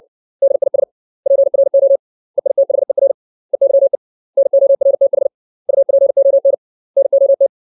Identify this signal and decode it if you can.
Morse code, heard as a keyed tone, about 60 wpm, 550 Hz. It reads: DIS PRO EITHER EYE NORTH UPON NOT
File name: 632 words / 632 Words 60WPM